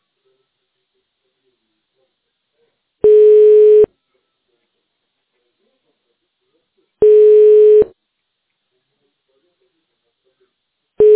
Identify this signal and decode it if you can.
Ringback tone (425 Hz, 1 s on, 3 s off).